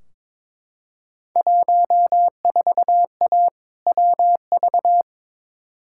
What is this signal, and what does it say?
Morse code, heard as a keyed tone, about 22 wpm, 700 Hz. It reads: 14A WV
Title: Field Day Exchanges Encore - 22wpm